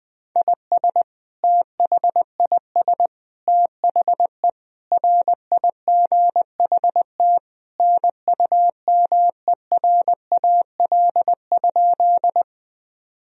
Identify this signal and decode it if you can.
Morse code, heard as a keyed tone, about 20 wpm, 700 Hz. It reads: IS THIS THE RIGHT NUMERAL?
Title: Sentences from Top 1000 Words - Review Part 2 - 20wpm